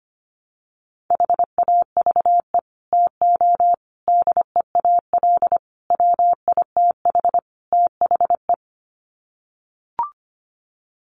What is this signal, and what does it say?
Morse code, heard as a keyed tone, about 25 wpm, 700 Hz. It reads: HAVE TO DEAL WITH THE
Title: Sets of 5 Words Encore2 25wpm